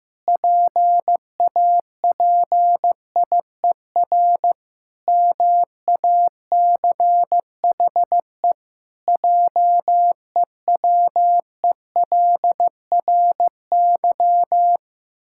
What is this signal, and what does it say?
Morse code, heard as a keyed tone, about 16 wpm, 700 Hz. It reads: PAPIER MACHE JEWELRY
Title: Pangram Challenge v2 - 15wpm